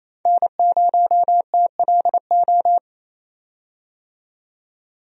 Morse code, heard as a keyed tone, about 28 wpm, 700 Hz. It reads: N0TLO
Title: USA Call Signs - 5 Characters - 28wpm